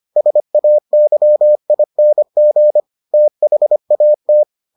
Morse code, heard as a keyed tone, about 25 wpm, 600 Hz. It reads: SAYING THAT